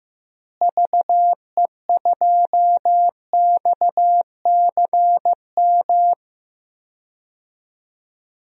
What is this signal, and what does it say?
Morse code, heard as a keyed tone, about 15 wpm, 700 Hz. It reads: VE2XCM
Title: DXCC-100 Call Signs - 6 Characters - Repeated 2x - 15wpm